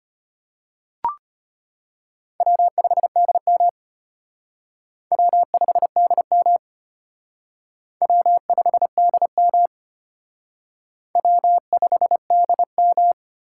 Morse code, heard as a keyed tone, about 32 wpm, 700 Hz. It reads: W5DM W5DM W5DM W5DM